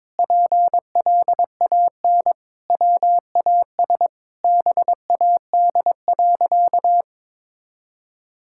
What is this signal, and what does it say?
Morse code, heard as a keyed tone, about 21 wpm, 700 Hz. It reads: PLAN WAS BAD.
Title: Sentences from Top 1000 Words - Review Part 6 - 22wpm